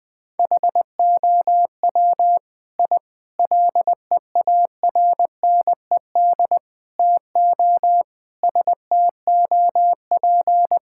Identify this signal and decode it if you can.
Morse code, heard as a keyed tone, about 20 wpm, 700 Hz. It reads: HOW I LEARNED TO STOP
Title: Sentences with 8-Word Phrases - 20wpm